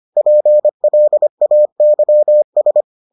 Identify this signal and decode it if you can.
Morse code, heard as a keyed tone, about 25 wpm, 600 Hz. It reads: PLAYS